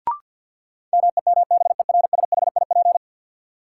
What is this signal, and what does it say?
Morse code, heard as a keyed tone, about 48 wpm, 700 Hz. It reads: MEMBERSHIP
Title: Sentences from Top 1400 Words - New Word Drill Alt - 50wpm